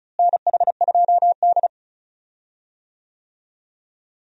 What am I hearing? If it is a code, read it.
Morse code, heard as a keyed tone, about 36 wpm, 700 Hz. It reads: NH2D
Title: USA Call Signs - 4 Characters - Repeated 3x - 35wpm